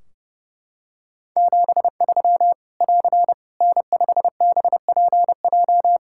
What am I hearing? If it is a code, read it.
Morse code, heard as a keyed tone, about 30 wpm, 700 Hz. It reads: 73 <AR> N5BPJ
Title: Practice QSO - Exchange 4 - Ending - 30wpm